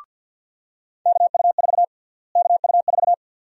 Morse code, heard as a keyed tone, about 46 wpm, 700 Hz. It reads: KU4 KU4
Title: USA Call Sign Prefixes - Repeated 2x - 50wpm